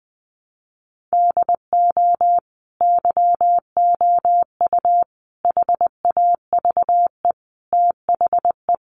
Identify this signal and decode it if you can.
Morse code, heard as a keyed tone, about 20 wpm, 700 Hz. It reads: DO YOU HAVE THE
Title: Sentences from Top 800 Words 20wpm